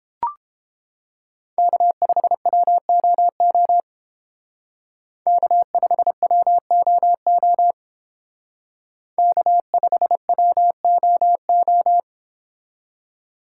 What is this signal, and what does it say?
Morse code, heard as a keyed tone, about 29 wpm, 700 Hz. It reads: K5WOO K5WOO K5WOO